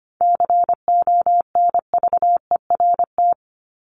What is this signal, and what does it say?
Morse code, heard as a keyed tone, about 25 wpm, 700 Hz. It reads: CONVERT
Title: Top 1751-2000 Words 25wpm